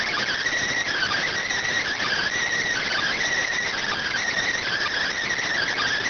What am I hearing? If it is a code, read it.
Emergency Alert System (SAME) tones (header bursts), heard as harsh buzzing bursts. It is heard from 0.9 to 5.1 s.